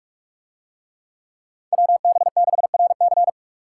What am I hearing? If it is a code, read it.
Morse code, heard as a keyed tone, about 45 wpm, 700 Hz. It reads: WB6RC